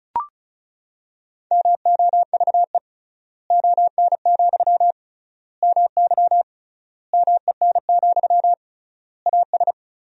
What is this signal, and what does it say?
Morse code, heard as a keyed tone, about 36 wpm, 700 Hz. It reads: MOVE ON, MY MEN, AS
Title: 3x Word-Spacing - Sentences from Top 500 Words - 35wpm